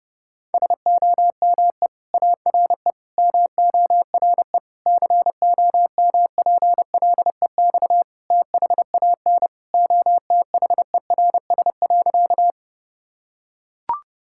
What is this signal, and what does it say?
Morse code, heard as a keyed tone, about 32 wpm, 700 Hz. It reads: SOME ARE MORE COMPLEX THAN OTHERS.
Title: Sentences from Top 1000 Words - Review Part 6 - 30wpm